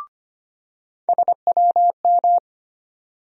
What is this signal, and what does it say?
Morse code, heard as a keyed tone, about 25 wpm, 700 Hz. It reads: SWM
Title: USA Call Sign Postfixes - 3 Letters 25wpm